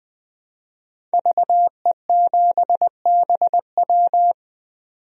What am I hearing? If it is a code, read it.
Morse code, heard as a keyed tone, about 20 wpm, 700 Hz. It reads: VE7BW
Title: DXCC-100 Call Signs - 5 Characters - 20wpm